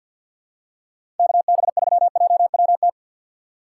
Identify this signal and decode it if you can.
Morse code, heard as a keyed tone, about 47 wpm, 700 Hz. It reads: KB3JWT
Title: USA Call Signs - 3x Character Spacing - 50wpm